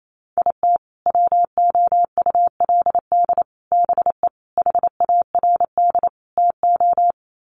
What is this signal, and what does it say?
Morse code, heard as a keyed tone, about 28 wpm, 700 Hz. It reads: IT WOULD BE HARD TO
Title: Sets of 5 Words Encore2 28wpm